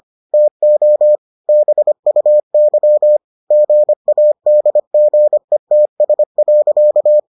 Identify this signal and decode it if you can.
Morse code, heard as a keyed tone, about 25 wpm, 600 Hz. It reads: TO BUY GADGETS.